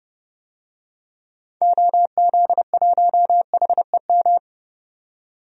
Morse code, heard as a keyed tone, about 30 wpm, 700 Hz. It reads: OZ1HEM
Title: DXCC-100 Call Signs - 6 Characters - 30wpm